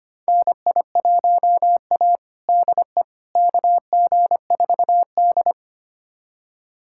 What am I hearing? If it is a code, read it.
Morse code, heard as a keyed tone, about 25 wpm, 700 Hz. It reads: NI1A DE KG4D
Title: USA Callsign DE Callsign - 4 Characters - 25wpm